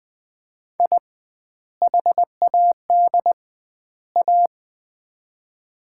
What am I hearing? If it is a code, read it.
Morse code, heard as a keyed tone, about 21 wpm, 700 Hz. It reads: I HAD A
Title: 2x Word-Spacing - Sets of 3 Words - 20wpm